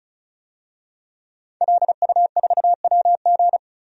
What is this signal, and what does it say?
Morse code, heard as a keyed tone, about 35 wpm, 700 Hz. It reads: LU4WG
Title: DXCC-100 Call Signs - 5 Characters - Repeated 3x - 35wpm